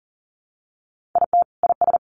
Morse code, heard as a keyed tone, about 40 wpm, 700 Hz. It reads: IT IS
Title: Sentences from Top 1400 Words 40wpm